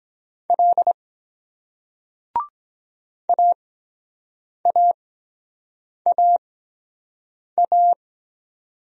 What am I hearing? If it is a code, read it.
Morse code, heard as a keyed tone, about 23 wpm, spaced as for 7 wpm, 700 Hz. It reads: L AAAA